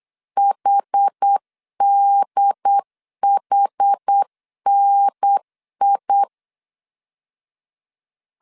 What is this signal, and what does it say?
Morse code, heard as a keyed tone, about 8 wpm, 800 Hz. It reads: HDHNI